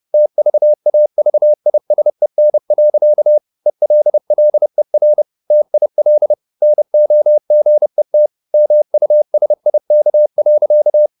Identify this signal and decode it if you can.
Morse code, heard as a keyed tone, about 30 wpm, 600 Hz. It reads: TVAVISEN. ELLER TIL NOGET MUSIK.
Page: Ferie 30 wpm